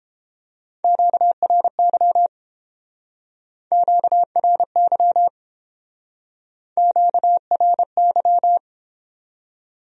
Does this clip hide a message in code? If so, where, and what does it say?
Yes: Morse code from 0.8 to 8.6 s, heard as a keyed tone, about 32 wpm, 700 Hz. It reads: QRY QRY QRY